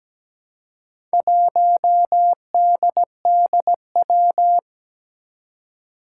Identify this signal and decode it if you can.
Morse code, heard as a keyed tone, about 17 wpm, 700 Hz. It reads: 1DDW
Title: Four Challenging Letter-Number Combinations 17wpm